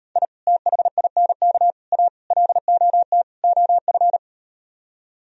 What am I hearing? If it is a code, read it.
Morse code, heard as a keyed tone, about 39 wpm, 700 Hz. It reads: I THINK A LOT OF